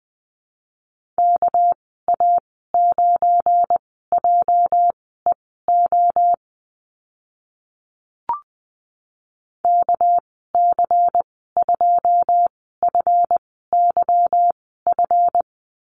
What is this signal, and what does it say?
Morse code, heard as a keyed tone, about 20 wpm, spaced as for 14 wpm, 700 Hz. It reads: KA9JEO KC2FYF
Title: 207 Lesson 33g - Letter j - Call Signs 20wpm